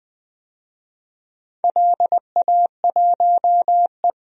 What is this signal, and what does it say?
Morse code, heard as a keyed tone, about 20 wpm, 700 Hz. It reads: LA1E
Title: DXCC-100 Call Signs - 4 Characters - Repeated 3x - 20wpm